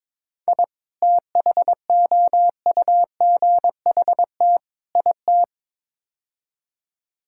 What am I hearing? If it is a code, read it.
Morse code, heard as a keyed tone, about 22 wpm, 700 Hz. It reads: I THOUGHT IT
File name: practice / 22wpm / Build A Phrase - 5-Word Phrases - 22wpm